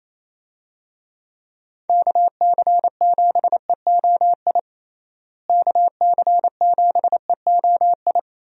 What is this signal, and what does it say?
Morse code, heard as a keyed tone, about 28 wpm, 700 Hz. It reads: KC7EOI KC7EOI
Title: USA Call Signs - 6 Characters - Repeated 2x - 28wpm